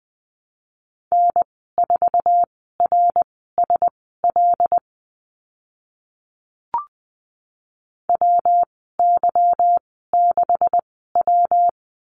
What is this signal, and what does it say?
Morse code, heard as a keyed tone, about 20 wpm, spaced as for 15 wpm, 700 Hz. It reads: N4RSL WY6W
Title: 147 Lesson 24g - Letter y - Call Signs 20wpm